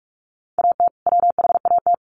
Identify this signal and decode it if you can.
Morse code, heard as a keyed tone, about 44 wpm, 700 Hz. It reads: AT WHAT
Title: Sentences from Top 800 Words 45wpm